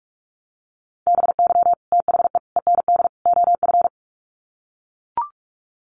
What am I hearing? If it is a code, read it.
Morse code, heard as a keyed tone, about 44 wpm, 700 Hz. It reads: BY THE END OF
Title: Sets of 4 Words 45wpm